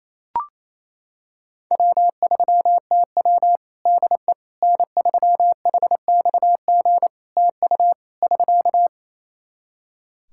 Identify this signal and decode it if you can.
Morse code, heard as a keyed tone, about 27 wpm, 700 Hz. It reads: W3TW DE N3HXG TU <SK>
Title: Practice QSO - Exchange 4 - Ending - 28wpm